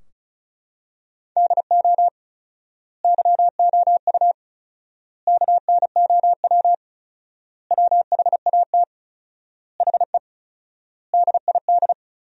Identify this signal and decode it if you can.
Morse code, heard as a keyed tone, about 34 wpm, 700 Hz. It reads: DO YOU KNOW WHAT HE DID
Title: 4x Word-Spacing - Sentences from Top 300 Words - 35wpm